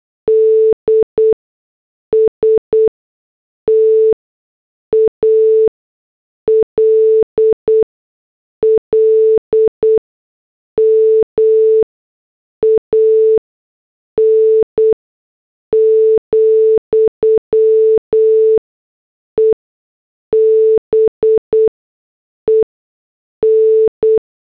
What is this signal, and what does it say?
Morse code, heard as a keyed tone, about 8 wpm, spaced as for 6 wpm, 440 Hz. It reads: D STALLMAN, EBEN
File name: morse / headlines